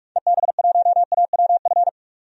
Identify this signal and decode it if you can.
Morse code, heard as a keyed tone, about 44 wpm, 690 Hz. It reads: EB1AWF
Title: DXCC-100 Call Signs - 6 Characters - 45wpm